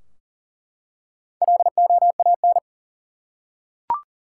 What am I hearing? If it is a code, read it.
Morse code, heard as a keyed tone, about 40 wpm, 700 Hz. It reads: LOAN
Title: Four Letter Words - 40wpm